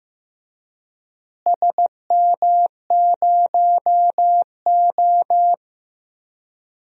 Morse code, heard as a keyed tone, about 16 wpm, 700 Hz. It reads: SM0O
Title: DXCC-100 Call Signs - 4 Characters - Repeated 2x - 15wpm